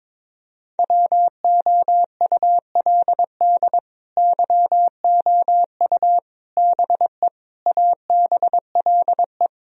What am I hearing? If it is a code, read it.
Morse code, heard as a keyed tone, about 22 wpm, 700 Hz. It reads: WOULD YOU BE ABLE
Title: Sentences from Top 1000 Words - Review Part 2 - 22wpm